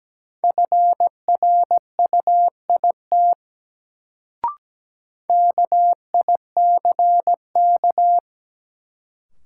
Morse code, heard as a keyed tone, about 17 wpm, 700 Hz. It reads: FRUIT KICK
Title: Sentences from Top 1300 Words - New Word Drill Alt - 17wpm